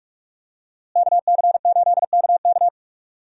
Morse code, heard as a keyed tone, about 46 wpm, 700 Hz. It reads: KC8KK
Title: USA Call Signs - 5 Characters - Repeated 2x - 45wpm